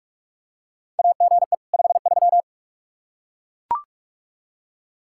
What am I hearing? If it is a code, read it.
Morse code, heard as a keyed tone, about 42 wpm, 700 Hz. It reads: AGE 53
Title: K1USN SST QSO Practice 45wpm